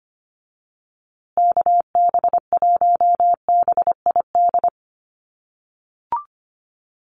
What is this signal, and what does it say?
Morse code, heard as a keyed tone, about 25 wpm, 700 Hz. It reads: KB1BID
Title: USA Call Signs Repeated 2x 25wpm